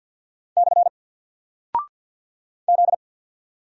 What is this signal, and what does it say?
Morse code, heard as a keyed tone, about 48 wpm, 700 Hz. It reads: / Z
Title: Single Character (Letter:Number:Punctuation:Prosign) Alt 50wpm